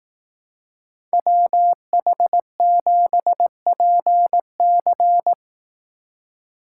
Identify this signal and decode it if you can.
Morse code, heard as a keyed tone, about 18 wpm, 700 Hz. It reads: WH7PC
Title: Speed Racing - USA 2x2 Call Signs 15wpm